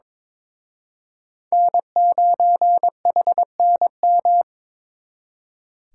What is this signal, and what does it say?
Morse code, heard as a keyed tone, about 26 wpm, 700 Hz. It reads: N9HNM
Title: Speed Racing - USA Call Signs 22wpm